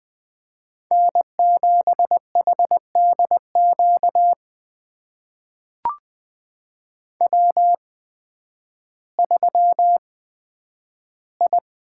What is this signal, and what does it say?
Morse code, heard as a keyed tone, about 20 wpm, spaced as for 11 wpm, 700 Hz. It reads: N7HDQ W3I
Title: USA Call Signs - 8x Character Spacing - 20wpm